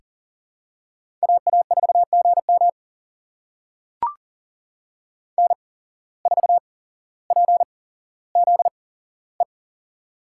Morse code, heard as a keyed tone, about 40 wpm, spaced as for 17 wpm, 700 Hz. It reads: AA4GM N4PZE
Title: USA Call Signs - 5 Characters - 8x Character Spacing - 40wpm